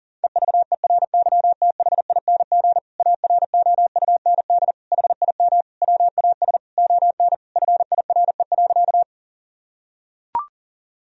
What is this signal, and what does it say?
Morse code, heard as a keyed tone, about 40 wpm, 700 Hz. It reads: EVERYTHING AROUND HIM WAS ON FIRE.
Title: Sentences from Top 1000 Words - Review Part 3 - 40wpm